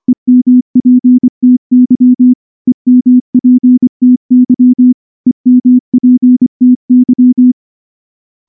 Morse code, heard as a keyed tone, about 24 wpm, 250 Hz. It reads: EMPTY EMPTY EMPTY
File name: morse-training-25WPM